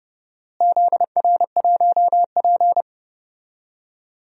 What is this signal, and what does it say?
Morse code, heard as a keyed tone, about 30 wpm, 700 Hz. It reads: ZR1P
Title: DXCC-100 Call Signs - 4 Characters - Repeated 3x - 30wpm